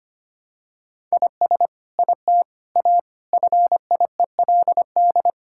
Morse code, heard as a keyed tone, about 25 wpm, 700 Hz. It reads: IS IT A FIELD
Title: Sentences from Top 1000 Words - Review Part 5 - 25wpm